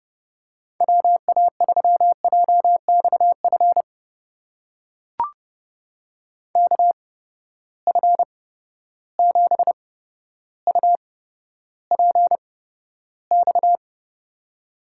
Morse code, heard as a keyed tone, about 30 wpm, spaced as for 13 wpm, 700 Hz. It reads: WA3JXF KF7UPX
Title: USA Call Signs - 6 Characters - 8x Character Spacing - 30wpm